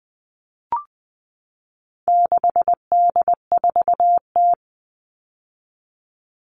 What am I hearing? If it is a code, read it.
Morse code, heard as a keyed tone, about 19 wpm, 700 Hz. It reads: 6D4T
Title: All Cut Numbers - 4-Digit 20wpm